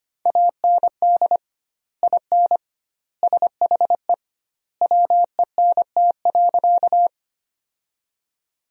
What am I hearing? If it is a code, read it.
Morse code, heard as a keyed tone, about 25 wpm, 700 Hz. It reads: AND IN SHE WENT.
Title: 2x Word-Spacing - Sentences from Top 200 Words - 25wpm